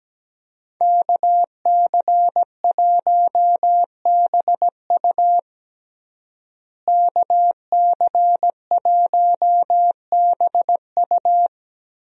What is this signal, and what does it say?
Morse code, heard as a keyed tone, about 17 wpm, 700 Hz. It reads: KC1BU KC1BU
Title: USA Call Signs - 5 Characters - Repeated 3x - 17wpm